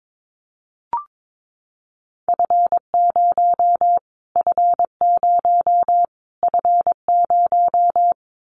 Morse code, heard as a keyed tone, about 22 wpm, 700 Hz. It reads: F0 F0 F0
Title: Callsign Prefixes for Top 25 DXCC - Repeated 3x - 22wpm